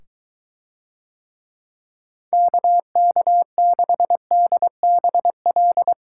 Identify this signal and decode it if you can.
Morse code, heard as a keyed tone, about 23 wpm, 700 Hz. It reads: KK6DBL